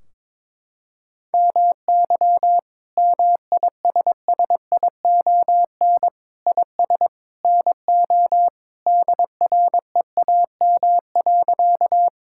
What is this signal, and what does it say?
Morse code, heard as a keyed tone, about 22 wpm, 700 Hz. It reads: MY MISSION IS NO DREAM.
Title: Sentences from Top 1600 Words 22wpm